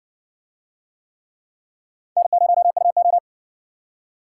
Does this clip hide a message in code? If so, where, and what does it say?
Yes: Morse code from 2.2 to 3.2 s, heard as a keyed tone, about 60 wpm, 700 Hz. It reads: N0UO